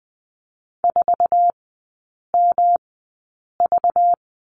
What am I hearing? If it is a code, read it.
Morse code, heard as a keyed tone, about 20 wpm, spaced as for 8 wpm, 690 Hz. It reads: 4MV
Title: 191 Lesson 31e - Letter v - Three Characters 20wpm